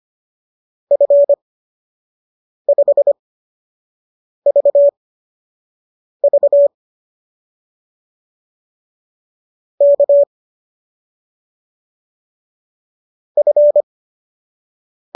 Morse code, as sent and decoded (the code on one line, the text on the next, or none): ..-. ..... ...- ...- / -.- / ..-.
F5VV K F